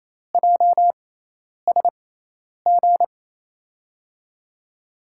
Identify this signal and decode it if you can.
Morse code, heard as a keyed tone, about 28 wpm, spaced as for 10 wpm, 700 Hz. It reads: JSG
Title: USA Call Signs - 5 Characters - 6x Character Spacing - 28wpm